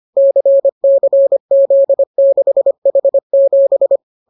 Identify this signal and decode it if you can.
Morse code, heard as a keyed tone, about 25 wpm, 550 Hz. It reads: CCZ6H7